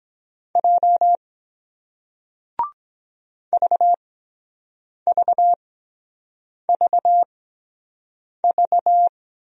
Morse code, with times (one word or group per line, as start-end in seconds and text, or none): 0.6-1.2 J
3.5-9.1 VVVV